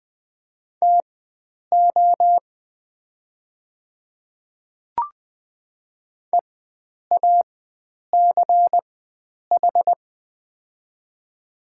Morse code, heard as a keyed tone, about 21 wpm, spaced as for 9 wpm, 700 Hz. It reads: TO EACH
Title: 4x Character Spacing - Top 100 Words - 20wpm